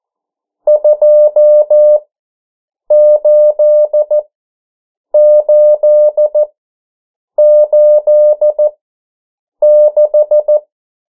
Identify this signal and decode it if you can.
Morse code, heard as a keyed tone, about 14 wpm, spaced as for 7 wpm, 600 Hz. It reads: 28886
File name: morse mp3 / 09 urok 8